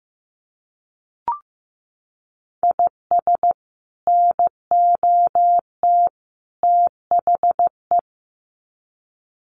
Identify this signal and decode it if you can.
Morse code, heard as a keyed tone, about 15 wpm, 700 Hz. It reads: IS NOT THE
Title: Sets of 3 Words 15wpm